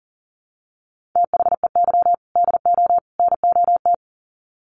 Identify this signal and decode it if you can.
Morse code, heard as a keyed tone, about 40 wpm, 700 Hz. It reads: THEY DO NOT
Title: Sets of 3 Words 40wpm